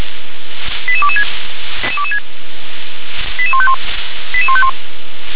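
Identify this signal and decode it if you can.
Five-tone selective call (ZVEI), heard as quick runs of five tones, tones of about 70 ms. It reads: 88226 88151 88151
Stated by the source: ZVEI five-tone selective call received on SDR, code 88007.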